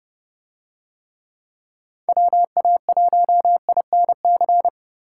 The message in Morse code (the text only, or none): WA1INC